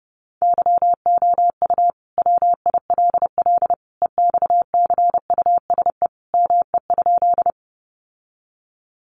Morse code, heard as a keyed tone, about 28 wpm, 700 Hz. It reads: YOU WILL EXCUSE ME?
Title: Sentences from Top 700 Words 30wpm